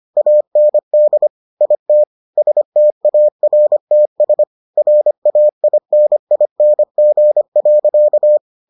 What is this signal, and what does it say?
Morse code, heard as a keyed tone, about 25 wpm, 600 Hz. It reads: AND IT STARTS RAINING.